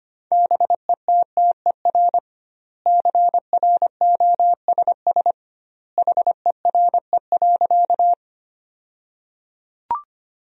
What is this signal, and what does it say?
Morse code, heard as a keyed tone, about 25 wpm, 700 Hz. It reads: BETTER CROSS HERE.
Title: 2x Word-Spacing - Sentences from Top 400 Words - 25wpm